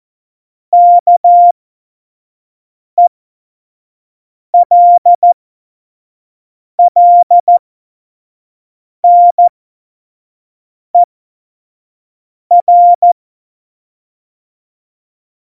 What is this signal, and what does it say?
Morse code, heard as a keyed tone, about 14 wpm, spaced as for 5 wpm, 700 Hz. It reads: KELLNER